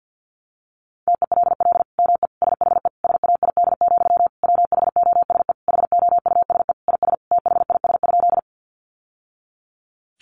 Morse code, heard as a keyed tone, about 50 wpm, 700 Hz. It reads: TELL ME SHE SAID, WHOSE HOUSE IS THIS?
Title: Sentences from Top 1200 Words 50wpm